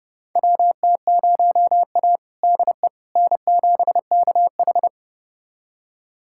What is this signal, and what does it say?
Morse code, heard as a keyed tone, about 30 wpm, 700 Hz. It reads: WT0A DE N7KH
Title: USA Callsign DE Callsign - 4 Characters - 30wpm